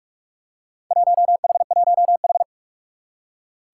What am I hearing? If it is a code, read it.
Morse code, heard as a keyed tone, about 46 wpm, 700 Hz. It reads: 1H1H
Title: Four Challenging Letter-Number Combinations 45wpm